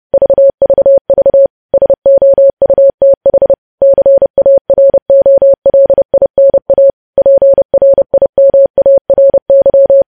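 Morse code, heard as a keyed tone, about 30 wpm, 550 Hz. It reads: VVV SOUTH CAROLINA PRIMARY
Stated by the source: News headlines sent at 30 wpm.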